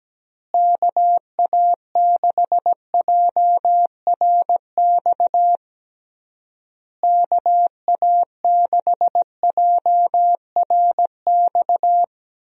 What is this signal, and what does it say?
Morse code, heard as a keyed tone, about 17 wpm, 700 Hz. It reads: KA6JRX KA6JRX
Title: USA Call Signs - 6 Characters - Repeated 3x - 17wpm